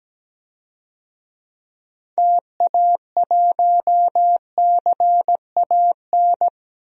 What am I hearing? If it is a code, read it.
Morse code, heard as a keyed tone, about 17 wpm, 700 Hz. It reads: TA1CAN
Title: DXCC-100 Call Signs - 6 Characters - 17wpm